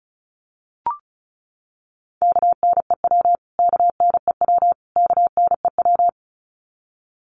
Morse code, heard as a keyed tone, about 34 wpm, 700 Hz. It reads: KNEW KNEW KNEW
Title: Top 500 Words Repeated 35wpm